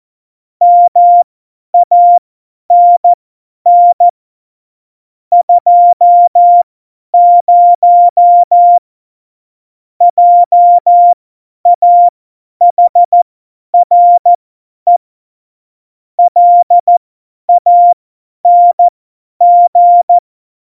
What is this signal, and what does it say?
Morse code, heard as a keyed tone, about 14 wpm, spaced as for 10 wpm, 700 Hz. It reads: MANN 20 JAHRE LANG
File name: Morsekurs / Witze / Witz 11. Februar 2020, 50 F70